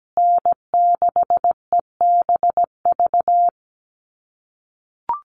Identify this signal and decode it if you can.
Morse code, heard as a keyed tone, about 17 wpm, 700 Hz. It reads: N6EBV
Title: USA Call Signs Repeated 3x 17wpm